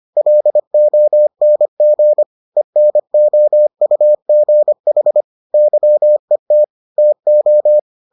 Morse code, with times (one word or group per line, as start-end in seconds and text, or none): 0.2-2.2 LONG
2.6-5.2 ENOUGH
5.5-6.6 YET
7.0-7.8 TO